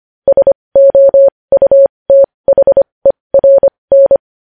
Morse code, heard as a keyed tone, about 25 wpm, spaced as for 20 wpm, 550 Hz. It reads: SOUTHERN
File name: podcast / News headlines-1x-25x20